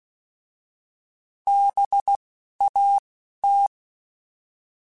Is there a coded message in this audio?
Yes: Morse code from 1.5 to 3.7 s, heard as a keyed tone, about 16 wpm, spaced as for 11 wpm, 800 Hz. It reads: BAT